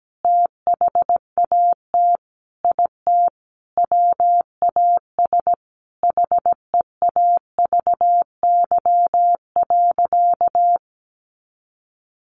Morse code, heard as a keyed tone, about 17 wpm, 700 Hz. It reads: THAT IT WAS HEAVY.
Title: Sentences from Top 1100 Words 17wpm